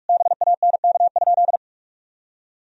Morse code, heard as a keyed tone, about 43 wpm, 700 Hz. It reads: BANK?
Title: Sentences from Top 1000 Words - Review Part 4 - 45wpm